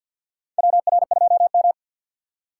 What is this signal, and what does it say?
Morse code, heard as a keyed tone, about 51 wpm, 700 Hz. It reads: WR2M
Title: USA Call Signs - 4 Characters - 3x Character Spacing - 50wpm